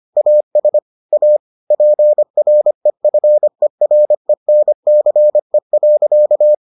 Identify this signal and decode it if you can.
Morse code, heard as a keyed tone, about 25 wpm, 600 Hz. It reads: AS A PREFERENCE.